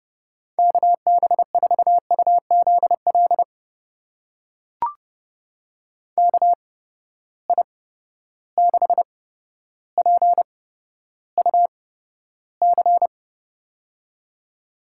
Morse code, heard as a keyed tone, about 30 wpm, spaced as for 13 wpm, 700 Hz. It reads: KB4UZL KI6PUC
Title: USA Call Signs - 6 Characters - 8x Character Spacing - 30wpm